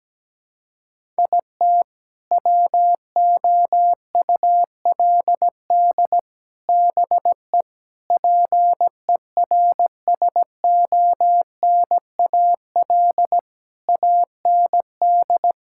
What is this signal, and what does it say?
Morse code, heard as a keyed tone, about 17 wpm, 700 Hz. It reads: IT WOULD BE PERSONAL AND
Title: Sentences from Top 1000 Words - Review Part 5 - 17wpm